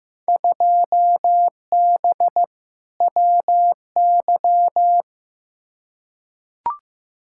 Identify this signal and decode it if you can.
Morse code, heard as a keyed tone, about 15 wpm, 700 Hz. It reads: 2B WY
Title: Field Day Exchanges Encore - 15wpm